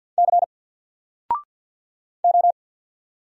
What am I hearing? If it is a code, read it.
Morse code, heard as a keyed tone, about 48 wpm, 700 Hz. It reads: C O
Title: Single Letter-Number Alt 50wpm